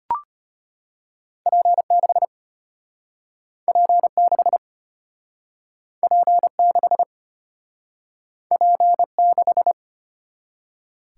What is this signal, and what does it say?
Morse code, heard as a keyed tone, about 32 wpm, 700 Hz. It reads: P6 P6 P6 P6